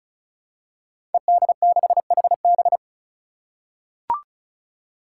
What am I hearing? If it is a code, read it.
Morse code, heard as a keyed tone, about 36 wpm, 700 Hz. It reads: ED6HB
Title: Five Challenging Letter-Number Combinations 35wpm